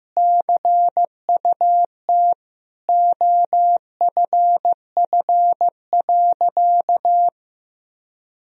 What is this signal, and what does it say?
Morse code, heard as a keyed tone, about 15 wpm, 700 Hz. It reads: CUT OFF.
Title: Sentences from Top 1000 Words - Review Part 1 - 15wpm